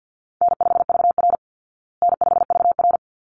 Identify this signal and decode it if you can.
Morse code, heard as a keyed tone, about 49 wpm, 700 Hz. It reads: N5VR N5VR
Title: Easy Call Signs Repeated 2x - 50wpm